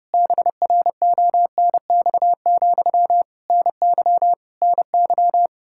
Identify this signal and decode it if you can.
Morse code, heard as a keyed tone, about 30 wpm, 700 Hz. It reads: BRONX, NY NY
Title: Practice QSO - Exchange 1 - Beginning - 30wpm